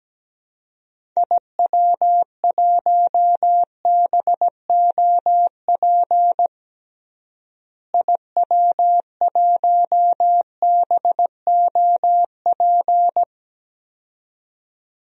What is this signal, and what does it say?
Morse code, heard as a keyed tone, about 17 wpm, 700 Hz. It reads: IW1BOP IW1BOP
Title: DXCC-100 Call Signs - 6 Characters - Repeated 2x - 17wpm